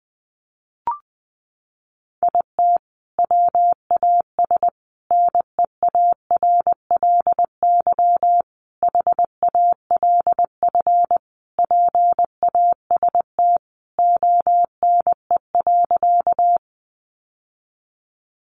Morse code, heard as a keyed tone, about 19 wpm, 700 Hz. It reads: IT WAS NEARLY HALF PAST ONE.
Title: Sentences from Top 1300 Words 20wpm